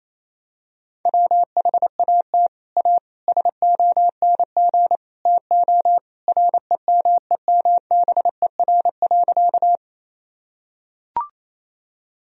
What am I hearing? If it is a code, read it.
Morse code, heard as a keyed tone, about 27 wpm, 700 Hz. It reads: WHAT A SONG TO REMEMBER.
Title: Sentences from Top 1000 Words - Review Part 2 - 28wpm